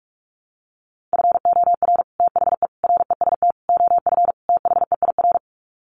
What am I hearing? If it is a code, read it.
Morse code, heard as a keyed tone, about 44 wpm, 700 Hz. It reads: FOR THE REST OF THEIR
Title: Sets of 5 Words Encore2 45wpm